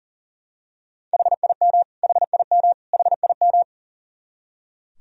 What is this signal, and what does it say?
Morse code, heard as a keyed tone, about 40 wpm, 700 Hz. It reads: HIM HIM HIM
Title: Top 100 Words Repeated 40wpm